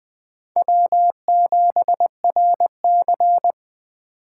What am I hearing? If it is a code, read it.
Morse code, heard as a keyed tone, about 18 wpm, 700 Hz. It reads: W7RC
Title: Speed Racing - USA 1x2 Call Signs 15wpm